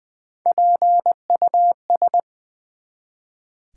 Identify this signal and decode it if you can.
Morse code, heard as a keyed tone, about 20 wpm, 700 Hz. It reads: PUS
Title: Build A Word - 1 Letter At A Time - 3 to 5-Letter Words - 20wpm